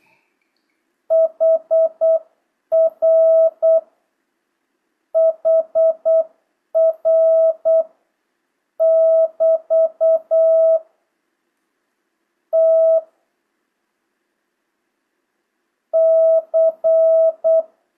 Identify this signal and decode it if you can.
Morse code, heard as a keyed tone, about 8 wpm, 650 Hz. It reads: HR HR <BT> T C
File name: jh8cbh / 2025cw